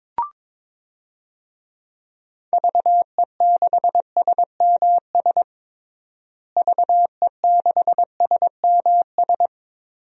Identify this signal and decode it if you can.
Morse code, heard as a keyed tone, about 22 wpm, 700 Hz. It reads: VE6SMS VE6SMS
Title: DXCC-100 Call Signs - 6 Characters - Repeated 2x - 22wpm